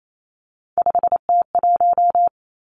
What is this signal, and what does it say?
Morse code, heard as a keyed tone, about 28 wpm, 700 Hz. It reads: 5T1
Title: Three Challenging Letter-Number Combinations 28wpm